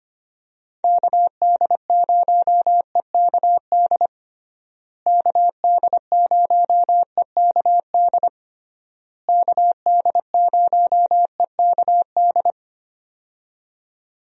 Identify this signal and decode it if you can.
Morse code, heard as a keyed tone, about 25 wpm, 700 Hz. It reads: KD0EKD KD0EKD KD0EKD
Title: USA Call Signs - 6 Characters - Repeated 3x - 25wpm